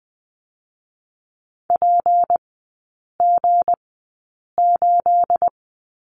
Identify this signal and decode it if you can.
Morse code, heard as a keyed tone, about 20 wpm, spaced as for 8 wpm, 700 Hz. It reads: PG8
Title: 218 Lesson 35e - Number 0 - Three Characters 20wpm